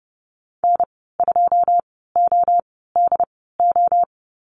Morse code, heard as a keyed tone, about 30 wpm, spaced as for 17 wpm, 700 Hz. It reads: N2ODO
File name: Lesson 14g - Number 2 - Call Signs 30wpm